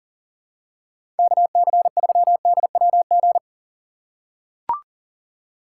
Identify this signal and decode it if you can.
Morse code, heard as a keyed tone, about 37 wpm, 700 Hz. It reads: KC3DWG
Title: USA Call Signs - 6 Characters - 2x Character Spacing - 40wpm